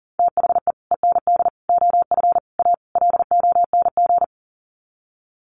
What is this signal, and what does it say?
Morse code, heard as a keyed tone, about 40 wpm, 700 Hz. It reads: THE END OF A LONG
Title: Sets of 5 Words Encore2 40wpm